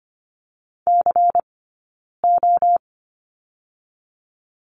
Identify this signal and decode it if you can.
Morse code, heard as a keyed tone, about 25 wpm, 700 Hz. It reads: C O
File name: course / 25:10wpm / 099 Lesson 17d - Letter u - Two Characters 25wpm